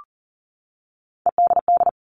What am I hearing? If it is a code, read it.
Morse code, heard as a keyed tone, about 40 wpm, 700 Hz. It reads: EDD
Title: All Cut Numbers - 3-Digit 40wpm